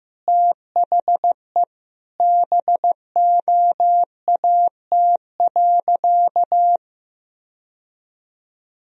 Morse code, heard as a keyed tone, about 15 wpm, 700 Hz. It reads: THE BOAT.
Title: Sentences from Top 1000 Words - Review Part 3 - 15wpm